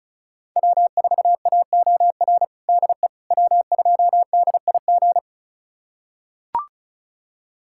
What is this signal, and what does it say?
Morse code, heard as a keyed tone, about 36 wpm, 700 Hz. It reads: W4AOR DE W2DIG
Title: USA Callsign DE Callsign - 5 Characters - 35wpm